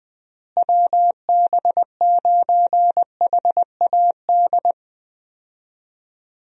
Morse code, heard as a keyed tone, about 20 wpm, 700 Hz. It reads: WB9HAD
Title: USA Call Signs - 6 Characters - Repeated 3x - 20wpm